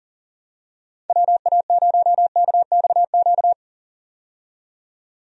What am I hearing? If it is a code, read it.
Morse code, heard as a keyed tone, about 37 wpm, 700 Hz. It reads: WA0KXQ
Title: USA Call Signs - 6 Characters - 40wpm